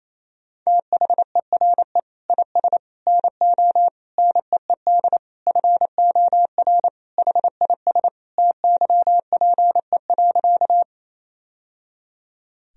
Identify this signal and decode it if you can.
Morse code, heard as a keyed tone, about 28 wpm, 700 Hz. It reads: THERE IS NO NEED FOR HIS TYPE.
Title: Sentences from Top 1000 Words - Review Part 3 - 28wpm